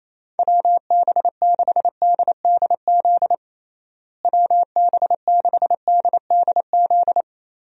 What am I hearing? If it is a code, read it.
Morse code, heard as a keyed tone, about 28 wpm, 700 Hz. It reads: WB6DDZ WB6DDZ
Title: USA Call Signs - 6 Characters - Repeated 3x - 28wpm